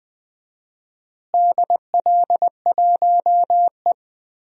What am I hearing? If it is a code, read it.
Morse code, heard as a keyed tone, about 21 wpm, 700 Hz. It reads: DL1E
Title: DXCC-100 Call Signs - 4 Characters - Repeated 2x - 20wpm